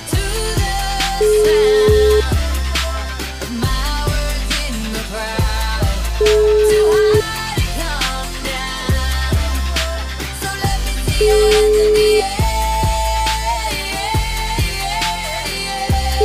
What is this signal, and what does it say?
Ringback tone (Europe, 425 Hz), with noise.